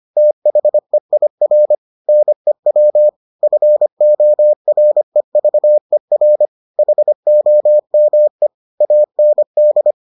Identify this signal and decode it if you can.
Morse code, heard as a keyed tone, about 25 wpm, 600 Hz. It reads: THEIR NEW FOREVER HOME AND